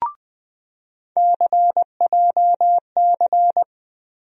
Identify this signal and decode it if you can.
Morse code, heard as a keyed tone, about 20 wpm, 700 Hz. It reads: CJC
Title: USA Call Sign Postfixes - 3 Letters 20wpm